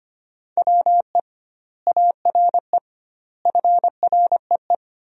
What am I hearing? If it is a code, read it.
Morse code, heard as a keyed tone, about 25 wpm, 700 Hz. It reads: WE ARE FREE
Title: 2x Word-Spacing - Sentences from Top 500 Words - 25wpm